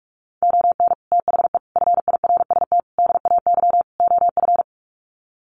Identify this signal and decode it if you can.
Morse code, heard as a keyed tone, about 44 wpm, 700 Hz. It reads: ON THE FIRST DAY OF
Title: Sets of 5 Words Encore 45wpm